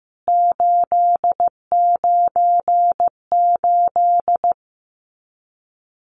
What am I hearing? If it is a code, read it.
Morse code, heard as a keyed tone, about 15 wpm, 700 Hz. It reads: 898
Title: CWT Exchange 15wpm